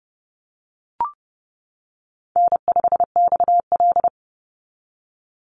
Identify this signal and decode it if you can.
Morse code, heard as a keyed tone, about 30 wpm, 700 Hz. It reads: N5XL
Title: NA SOTA Chaser Callsigns 30wpm